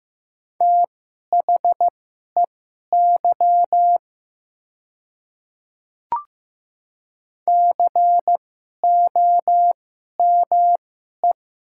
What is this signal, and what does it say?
Morse code, heard as a keyed tone, about 15 wpm, spaced as for 11 wpm, 700 Hz. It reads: THEY COME
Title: 2x Character Spacing - Top 100 Words - 15wpm